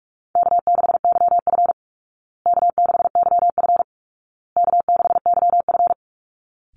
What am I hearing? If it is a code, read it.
Morse code, heard as a keyed tone, about 44 wpm, 700 Hz. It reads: K6YF K6YF K6YF
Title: Easy Call Signs Repeated 3x - 45wpm